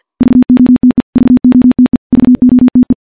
Morse code, heard as a keyed tone, about 51 wpm, 250 Hz. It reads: VOTE VOTE VOTE
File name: morse / morse-training-50WPM